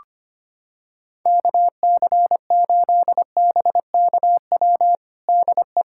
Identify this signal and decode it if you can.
Morse code, heard as a keyed tone, about 25 wpm, 700 Hz. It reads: KC8BKW DE
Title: USA Callsign DE Callsign - 6 Characters - 25wpm